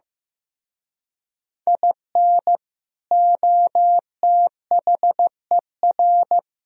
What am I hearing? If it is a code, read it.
Morse code, heard as a keyed tone, about 15 wpm, 700 Hz. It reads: IN OTHER
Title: Speed-Racing - Sets of 5 Words 15wpm